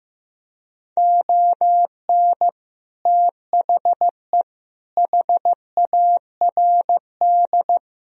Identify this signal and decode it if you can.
Morse code, heard as a keyed tone, about 16 wpm, 700 Hz. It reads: ON THE HARD
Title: Sentences from Top 1000 Words - Review Part 2 - 15wpm